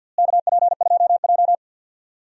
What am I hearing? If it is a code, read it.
Morse code, heard as a keyed tone, about 52 wpm, 700 Hz. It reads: KP2J
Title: DXCC-100 Call Signs - 4 Characters - Repeated 2x - 50wpm